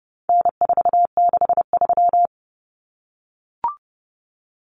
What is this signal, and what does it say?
Morse code, heard as a keyed tone, about 30 wpm, 700 Hz. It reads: N463
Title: Common Cut Numbers - 4-Digit 30wpm